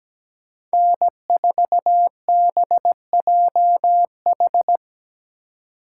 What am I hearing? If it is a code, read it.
Morse code, heard as a keyed tone, about 20 wpm, 700 Hz. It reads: N4BJH
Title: Speed Racing - USA 1x3 Call Signs 17wpm